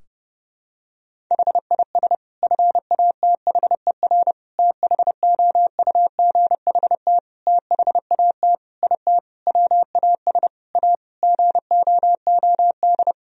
Morse code, heard as a keyed tone, about 30 wpm, 700 Hz. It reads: HIS FATHER THOUGHT THAT IT WAS A GOOD
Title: Sentences from Top 1000 Words - Review Part 4 - 30wpm